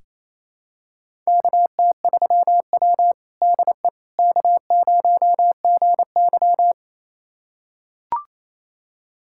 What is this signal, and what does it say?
Morse code, heard as a keyed tone, about 27 wpm, 700 Hz. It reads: KT3W DE K0GY
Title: USA Callsign DE Callsign - 4 Characters - 28wpm